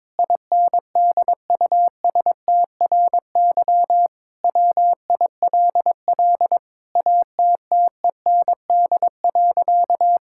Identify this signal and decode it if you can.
Morse code, heard as a keyed tone, about 22 wpm, 700 Hz. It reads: INDUSTRY WILL ATTEND.
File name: Sentences with 7-Word Phrases - 22wpm